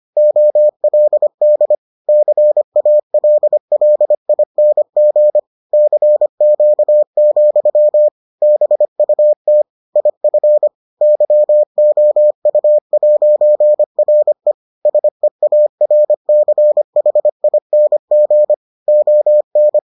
Morse code, heard as a keyed tone, about 25 wpm, 600 Hz. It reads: OLD CALLING CQ, BUT IF YOU'RE SEARCHING ON